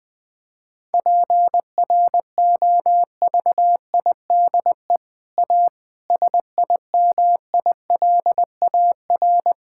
Morse code, heard as a keyed tone, about 20 wpm, 700 Hz. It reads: PROVIDE A SIMILAR
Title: Sentences from Top 1000 Words - Review Part 5 - 20wpm